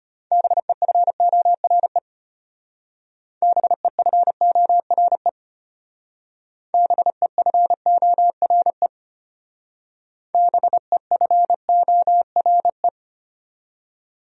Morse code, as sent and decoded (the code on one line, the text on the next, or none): -... ..-. --- .-. / -... ..-. --- .-. / -... ..-. --- .-. / -... ..-. --- .-. .
BEFORE BEFORE BEFORE BEFORE